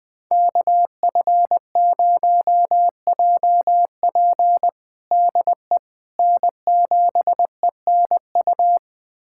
Morse code, heard as a keyed tone, about 20 wpm, 700 Hz. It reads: KF0JP DE N7ENU
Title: USA Callsign DE Callsign - 5 Characters - 20wpm